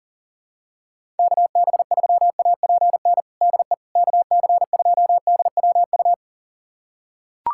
Morse code, heard as a keyed tone, about 38 wpm, 700 Hz. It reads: KB3APN DE KC2DWU
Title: USA Callsign DE Callsign - 6 Characters - 40wpm